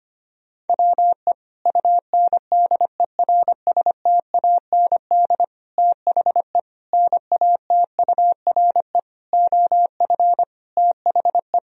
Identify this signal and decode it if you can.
Morse code, heard as a keyed tone, about 25 wpm, 700 Hz. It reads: WE UNDERSTAND THE NATURE OF THE
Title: Sentences from Top 1000 Words - Review Part 2 - 25wpm